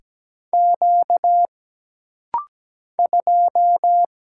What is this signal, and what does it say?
Morse code, heard as a keyed tone, about 17 wpm, 700 Hz. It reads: Q 2
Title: Single Letter-Number Alt - Rapid-Fire 17wpm